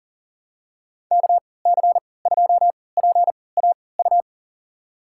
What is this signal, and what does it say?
Morse code, heard as a keyed tone, about 40 wpm, spaced as for 23 wpm, 700 Hz. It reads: KC2PAU
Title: USA Call Signs - 6 Characters - 3x Character Spacing - 40wpm